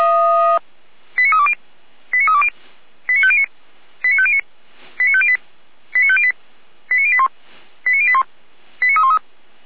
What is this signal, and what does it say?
Five-tone selective call (ZVEI), heard as quick runs of five tones, tones of about 70 ms. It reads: 89320 89320 89509 89509 89508 89508 89081 89081 89313